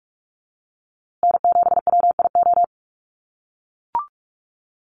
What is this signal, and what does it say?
Morse code, heard as a keyed tone, about 44 wpm, 700 Hz. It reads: N7WIO
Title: USA Call Signs Repeated 2x 45wpm